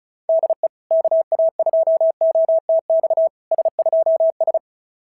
Morse code, heard as a keyed tone, about 36 wpm, 660 Hz. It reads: DE KA2OTX S2S
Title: SOTA QSO Practice 35wpm